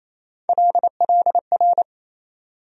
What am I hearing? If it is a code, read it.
Morse code, heard as a keyed tone, about 28 wpm, 700 Hz. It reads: LLR
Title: USA Call Sign Postfixes - 3 Letters 28wpm